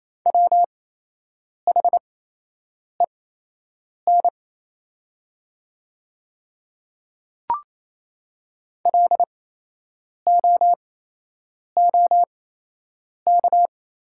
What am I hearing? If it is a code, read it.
Morse code, heard as a keyed tone, about 28 wpm, spaced as for 8 wpm, 700 Hz. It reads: WHEN LOOK
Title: 8x Character Spacing - Top 200 Words - 28wpm